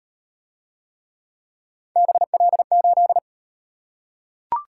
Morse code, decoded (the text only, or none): BL8